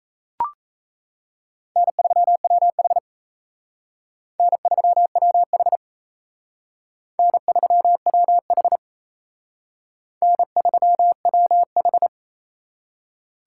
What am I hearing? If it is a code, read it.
Morse code, heard as a keyed tone, about 35 wpm, 700 Hz. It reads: N3WH N3WH N3WH N3WH